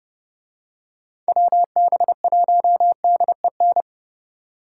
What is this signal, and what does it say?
Morse code, heard as a keyed tone, about 30 wpm, 700 Hz. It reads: WB1DEN